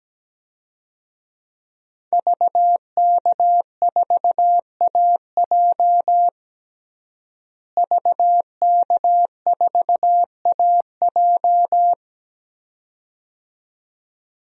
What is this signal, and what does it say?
Morse code, heard as a keyed tone, about 17 wpm, 700 Hz. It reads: VK4AJ VK4AJ
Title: DXCC-100 Call Signs - 5 Characters - Repeated 2x - 17wpm